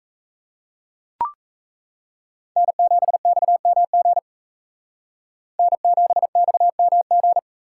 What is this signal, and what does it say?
Morse code, heard as a keyed tone, about 35 wpm, 700 Hz. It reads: N7XMG N7XMG